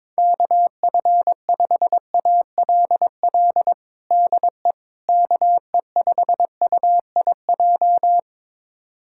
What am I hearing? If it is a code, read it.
Morse code, heard as a keyed tone, about 22 wpm, 700 Hz. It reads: KF5ALL DE KE5UIJ
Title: USA Callsign DE Callsign 22wpm